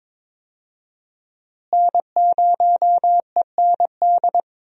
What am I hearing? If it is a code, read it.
Morse code, heard as a keyed tone, about 21 wpm, 700 Hz. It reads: N0END
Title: USA Call Signs - 5 Characters - Repeated 3x - 22wpm